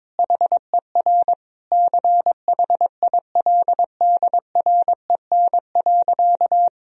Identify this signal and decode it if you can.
Morse code, heard as a keyed tone, about 21 wpm, 700 Hz. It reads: HER CHILDREN.
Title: Sentences from Top 1000 Words - Review Part 3 - 22wpm